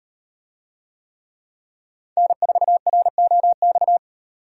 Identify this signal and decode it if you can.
Morse code, heard as a keyed tone, about 38 wpm, 700 Hz. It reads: N4ROX